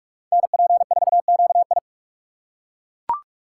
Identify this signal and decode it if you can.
Morse code, heard as a keyed tone, about 41 wpm, 700 Hz. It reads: NP4QI
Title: USA Call Signs - 5 Characters - 4x Character Spacing - 45wpm